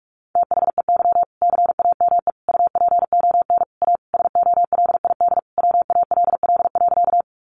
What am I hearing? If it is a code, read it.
Morse code, heard as a keyed tone, about 44 wpm, 700 Hz. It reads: THEY CAME UPON A SOLID WALL.
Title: Sentences from Top 1500 Words 45wpm